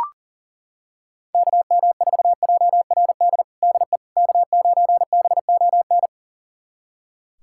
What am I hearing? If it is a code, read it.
Morse code, heard as a keyed tone, about 39 wpm, 700 Hz. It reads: KM4JRD DE K9BON
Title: USA Callsign DE Callsign 40wpm